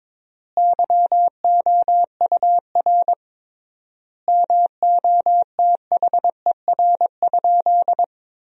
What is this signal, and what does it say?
Morse code, heard as a keyed tone, about 22 wpm, 700 Hz. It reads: YOUR MOTHER?
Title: 3x Word-Spacing - Sentences from Top 200 Words - 22wpm